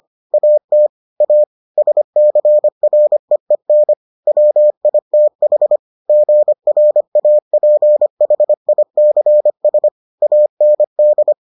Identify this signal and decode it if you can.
Morse code, heard as a keyed tone, about 25 wpm, 600 Hz. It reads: AT A SCREEN WITH GRAPHICS AND